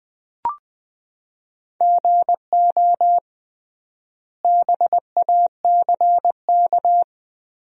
Morse code, heard as a keyed tone, about 20 wpm, 700 Hz. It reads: GO BACK
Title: 3x Word-Spacing - Sentences from Top 500 Words - 20wpm